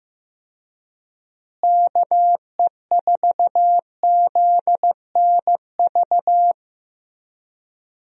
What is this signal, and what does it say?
Morse code, heard as a keyed tone, about 15 wpm, 700 Hz. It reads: KE4ZNV
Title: USA Call Signs - 6 Characters - Repeated 3x - 15wpm